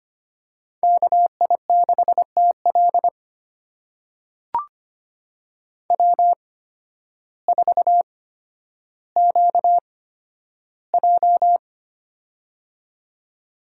Morse code, heard as a keyed tone, about 25 wpm, spaced as for 12 wpm, 700 Hz. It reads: KI6TL W4QJ
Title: USA Call Signs - 8x Character Spacing - 25wpm